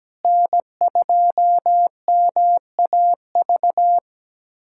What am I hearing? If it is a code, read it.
Morse code, heard as a keyed tone, about 17 wpm, 700 Hz. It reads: N2MAV
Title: USA Callsign DE Callsign - 5 Characters - 17wpm